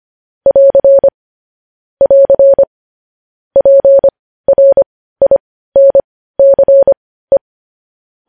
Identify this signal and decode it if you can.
Morse code, heard as a keyed tone, about 25 wpm, spaced as for 15 wpm, 550 Hz. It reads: <AR> <AR> PRINCE
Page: news headlines in morse code at 15 wpm